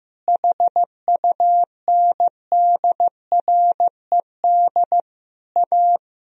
Morse code, heard as a keyed tone, about 16 wpm, 700 Hz. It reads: HUNDRED A
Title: Sentences from Top 1000 Words - Review Part 2 - 15wpm